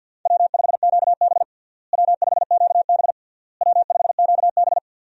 Morse code, heard as a keyed tone, about 52 wpm, 700 Hz. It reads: W5QB W5QB W5QB
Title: USA Call Signs - 4 Characters - Repeated 3x - 50wpm